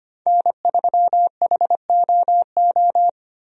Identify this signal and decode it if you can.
Morse code, heard as a keyed tone, about 25 wpm, 690 Hz. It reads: N3HOO
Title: USA Call Signs - 5 Characters - 25wpm